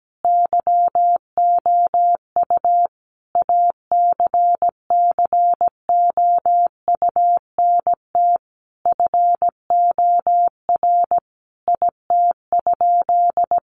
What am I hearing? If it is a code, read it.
Morse code, heard as a keyed tone, about 17 wpm, 700 Hz. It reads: YOU ACCOUNT FOR IT?
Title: Sentences from Top 900 Words 17wpm